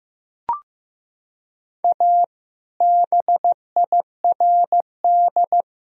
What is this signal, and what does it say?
Morse code, heard as a keyed tone, about 15 wpm, 700 Hz. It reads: A BIRD
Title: Sentences from Top 1000 Words - Review Part 2 - 15wpm